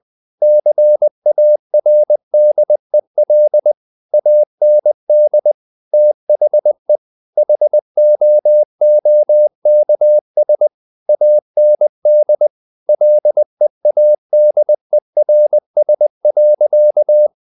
Morse code, heard as a keyed tone, about 20 wpm, 600 Hz. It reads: CARDEL AND THE HOOKS AND LEADERS.
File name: the old man and the sea / the old man and the sea part 1 20wpm